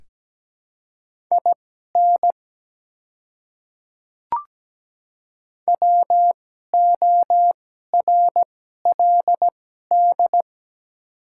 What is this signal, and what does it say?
Morse code, heard as a keyed tone, about 17 wpm, spaced as for 12 wpm, 690 Hz. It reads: IN WORLD
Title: 2x Character Spacing - Top 300 Words - 17wpm